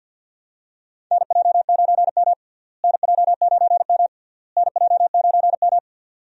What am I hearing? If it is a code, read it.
Morse code, heard as a keyed tone, about 50 wpm, 700 Hz. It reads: NJ9M NJ9M NJ9M
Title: USA Call Signs - 4 Characters - Repeated 3x - 50wpm